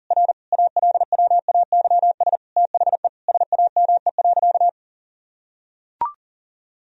Morse code, heard as a keyed tone, about 40 wpm, 700 Hz. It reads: R ALWAYS THE SAME.
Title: Sentences from Top 300 Words and Abbreviations 40wpm